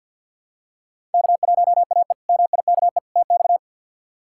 Morse code, heard as a keyed tone, about 51 wpm, 700 Hz. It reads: K1AE MIKE TX
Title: K1USN SST QSO Practice - 50wpm